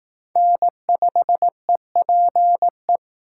Morse code, heard as a keyed tone, about 18 wpm, 700 Hz. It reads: N5EPE
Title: Speed Racing - USA 1x3 Call Signs 15wpm